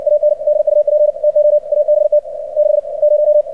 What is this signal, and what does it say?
Morse code, heard as a keyed tone, about 60 wpm, spaced as for 49 wpm, 600 Hz. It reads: AND UP TO ABT 60
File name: archived / CRSnet